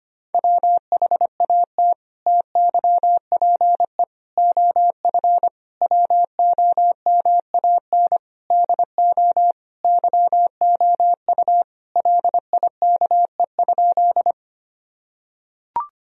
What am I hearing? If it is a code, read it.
Morse code, heard as a keyed tone, about 25 wpm, 700 Hz. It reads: WHAT TYPE OF WOMAN DO YOU LIKE?
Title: Sentences from Top 1000 Words - Review Part 3 - 25wpm